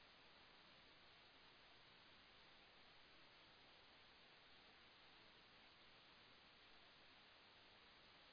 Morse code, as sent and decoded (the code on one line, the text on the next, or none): none